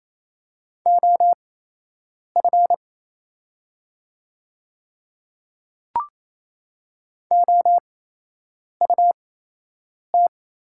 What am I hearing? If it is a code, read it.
Morse code, heard as a keyed tone, about 28 wpm, spaced as for 8 wpm, 700 Hz. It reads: OF OUT
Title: 8x Character Spacing - Top 100 Words - 28wpm